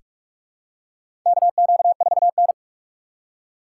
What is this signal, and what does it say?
Morse code, heard as a keyed tone, about 46 wpm, 700 Hz. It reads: KQ4N
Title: USA Call Signs - 4 Characters - 4x Character Spacing - 45wpm